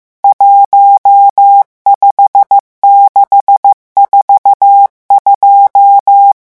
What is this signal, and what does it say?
Morse code, heard as a keyed tone, about 15 wpm, 800 Hz. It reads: 15642
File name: morse mp3 / revision11 chiffres vitesse 14